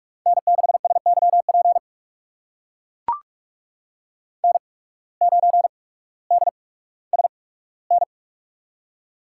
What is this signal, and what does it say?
Morse code, heard as a keyed tone, about 44 wpm, spaced as for 19 wpm, 700 Hz. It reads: N6SYP N9DSN
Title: USA Call Signs - 8x Character Spacing - 45wpm